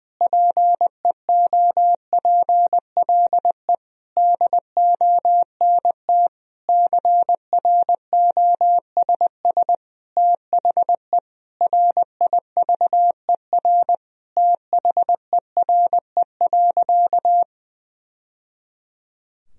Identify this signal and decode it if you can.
Morse code, heard as a keyed tone, about 20 wpm, 700 Hz. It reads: PEOPLE DONT CROSS THE RIVER THERE.
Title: Sentences from Top 1000 Words - Review Part 3 - 20wpm